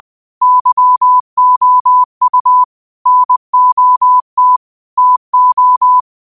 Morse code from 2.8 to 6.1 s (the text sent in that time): NOT TO